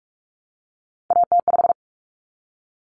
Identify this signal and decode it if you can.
Morse code, heard as a keyed tone, about 44 wpm, 700 Hz. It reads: AT5
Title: Common Cut Numbers - 3-Digit 45wpm